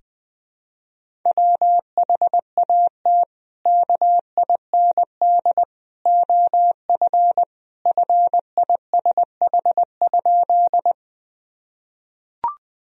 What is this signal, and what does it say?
Morse code, heard as a keyed tone, about 20 wpm, 700 Hz. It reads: WHAT KIND OF FISH?
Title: Sentences from Top 1000 Words - Review Part 1 - 20wpm